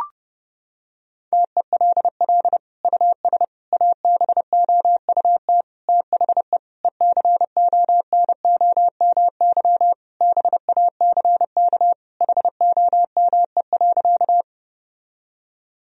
Morse code, heard as a keyed tone, about 32 wpm, 700 Hz. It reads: TELL US ABOUT THE ECONOMY BACK HOME.
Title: Sentences from Top 1000 Words - Review Part 3 - 30wpm